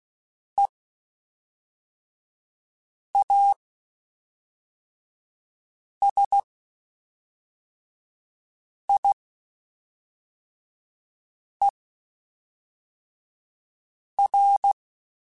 Morse code, heard as a keyed tone, about 16 wpm, spaced as for 3 wpm, 800 Hz. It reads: EASIER
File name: Quotes Spaced / Quotes Spaced-mp3s / Character sp @5